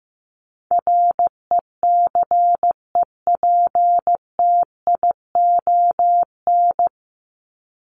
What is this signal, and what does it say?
Morse code, heard as a keyed tone, about 15 wpm, 700 Hz. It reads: RECEPTION
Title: Top 1751-2000 Words 15wpm